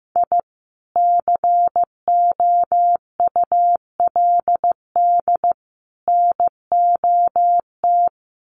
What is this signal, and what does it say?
Morse code, heard as a keyed tone, about 15 wpm, 700 Hz. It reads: I COULD NOT
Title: Sets of 3 Words 15wpm